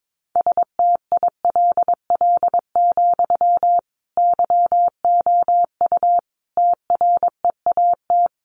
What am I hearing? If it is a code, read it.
Morse code, heard as a keyed tone, about 22 wpm, 700 Hz. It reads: STILL, YOU TREAT
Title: Sentences from Top 1300 Words 22wpm